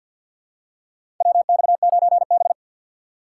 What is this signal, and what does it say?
Morse code, heard as a keyed tone, about 50 wpm, 700 Hz. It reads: WX9B
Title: USA Call Signs - 4 Characters - 3x Character Spacing - 50wpm